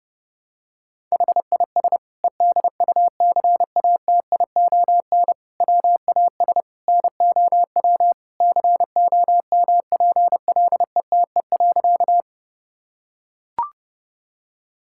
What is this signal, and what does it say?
Morse code, heard as a keyed tone, about 28 wpm, 700 Hz. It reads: HIS EDUCATION WAS NOW COMPLETE.
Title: Sentences from Top 1000 Words - Review Part 6 - 30wpm